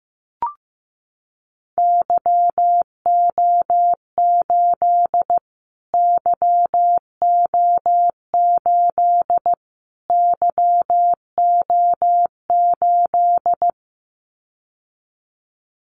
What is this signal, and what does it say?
Morse code, heard as a keyed tone, about 15 wpm, 700 Hz. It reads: YO8 YO8 YO8
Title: Callsign Prefixes for Top 25 DXCC - Repeated 3x - 15wpm